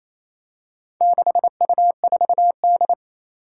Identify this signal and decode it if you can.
Morse code, heard as a keyed tone, about 28 wpm, 700 Hz. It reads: 6U4D
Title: Four Challenging Letter-Number Combinations 28wpm